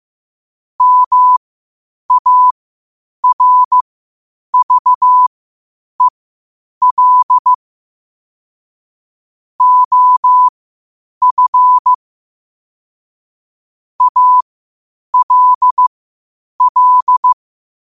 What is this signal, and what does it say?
Morse code, heard as a keyed tone, about 15 wpm, spaced as for 8 wpm, 1000 Hz. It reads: MARVEL OF ALL
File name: qotdfiles / 8wpm